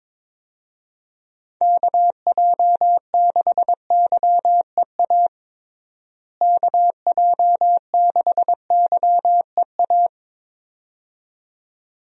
Morse code, heard as a keyed tone, about 22 wpm, 700 Hz. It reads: KJ6YEA KJ6YEA
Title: USA Call Signs - 6 Characters - Repeated 2x - 22wpm